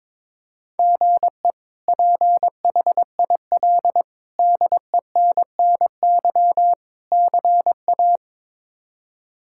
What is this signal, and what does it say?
Morse code, heard as a keyed tone, about 22 wpm, 700 Hz. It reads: GE PHIL DENNY CA
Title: K1USN SST QSO Practice - 22wpm